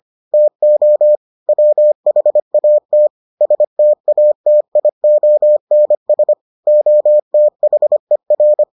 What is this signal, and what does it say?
Morse code, heard as a keyed tone, about 25 wpm, 600 Hz. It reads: TO WHAT STATIONS OTHER